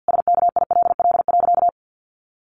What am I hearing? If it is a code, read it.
Morse code, heard as a keyed tone, about 50 wpm, 700 Hz. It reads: SKILL.
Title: Sentences from Top 1200 Words 50wpm